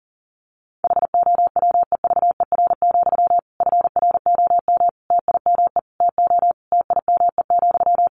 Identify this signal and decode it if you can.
Morse code, heard as a keyed tone, about 40 wpm, 700 Hz. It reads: HOWEVER, FROM TIME TO TIME,
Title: Sentences from Top 1300 Words 40wpm